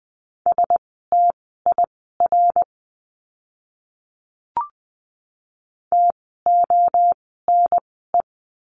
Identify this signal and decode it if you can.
Morse code, heard as a keyed tone, about 20 wpm, spaced as for 15 wpm, 700 Hz. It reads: STIR TONE
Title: 068 Lesson 12f - Letter d - Words 20wpm